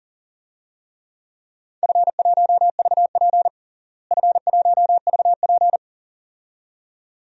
Morse code, heard as a keyed tone, about 40 wpm, 700 Hz. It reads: F1VP F1VP
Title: DXCC-100 Call Signs - 4 Characters - Repeated 2x - 40wpm